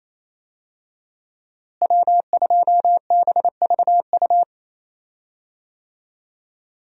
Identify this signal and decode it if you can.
Morse code, heard as a keyed tone, about 28 wpm, 700 Hz. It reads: W2BVU
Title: USA Call Signs - 5 Characters - Repeated 2x - 28wpm